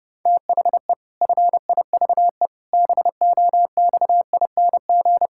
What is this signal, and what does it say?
Morse code, heard as a keyed tone, about 30 wpm, 700 Hz. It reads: THE FIVE BOXING
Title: Pangram Challenge v2 - 30wpm